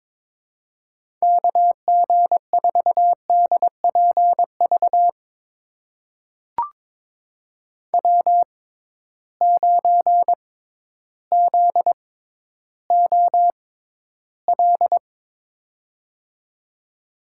Morse code, heard as a keyed tone, about 22 wpm, spaced as for 12 wpm, 700 Hz. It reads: KG4DPV W9ZOL
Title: USA Call Signs - 6x Character Spacing - 22wpm